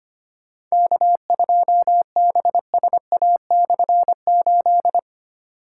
Morse code, heard as a keyed tone, about 25 wpm, 700 Hz. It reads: K2BSA/8
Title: K1USN SST QSO Practice - 25wpm